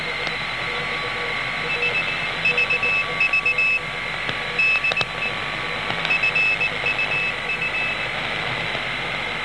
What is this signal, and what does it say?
Morse code, heard as a keyed tone, about 19 wpm, 2600 Hz, with noise. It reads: S VV DE FUU